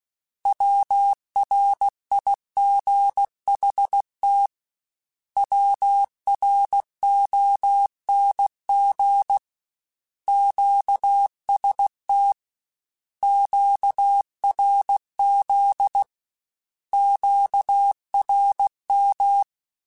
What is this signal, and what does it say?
Morse code, heard as a keyed tone, about 16 wpm, 800 Hz. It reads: WRIGHT WRONG QST QRZ QRM
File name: Short Runs / Short Runs-mp3s / Words @15